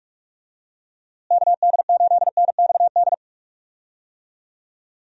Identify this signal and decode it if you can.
Morse code, heard as a keyed tone, about 46 wpm, 700 Hz. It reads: KD8NXD
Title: USA Call Signs - 6 Characters - Repeated 3x - 45wpm